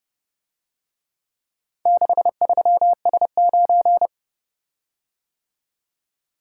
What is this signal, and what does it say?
Morse code, heard as a keyed tone, about 32 wpm, 700 Hz. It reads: 63S9